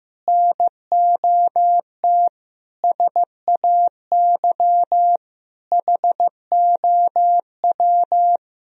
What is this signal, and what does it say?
Morse code, heard as a keyed tone, about 15 wpm, 700 Hz. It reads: NOT SAY HOW
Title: Sentences from Top 1000 Words - Review Part 1 - 15wpm